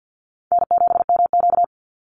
Morse code, heard as a keyed tone, about 50 wpm, 700 Hz. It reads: N7MQ
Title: Easy Call Signs - 50wpm